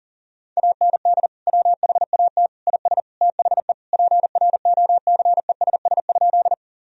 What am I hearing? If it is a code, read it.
Morse code, heard as a keyed tone, about 40 wpm, 700 Hz. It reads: AND WHAT IS THE PROCESS?
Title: Sentences from Top 1000 Words - Review Part 3 - 40wpm